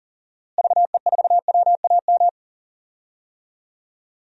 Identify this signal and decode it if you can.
Morse code, heard as a keyed tone, about 40 wpm, 700 Hz. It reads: VE4WAM
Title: DXCC-100 Call Signs - 6 Characters - Repeated 3x - 40wpm